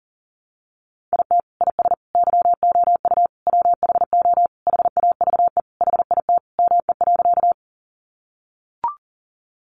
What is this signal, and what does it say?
Morse code, heard as a keyed tone, about 40 wpm, 700 Hz. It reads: IT IS YOU WHO HAVE HIT ME.
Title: Sentences from Top 1200 Words 40wpm